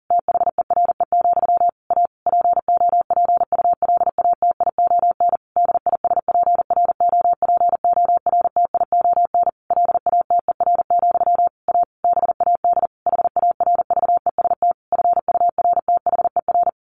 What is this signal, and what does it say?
Morse code, heard as a keyed tone, about 40 wpm, 700 Hz. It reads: THERE, A POPULATION DISPROPORTION LATER, A BAD HARVEST FURTHER
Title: A Sound Of Thunder 40wpm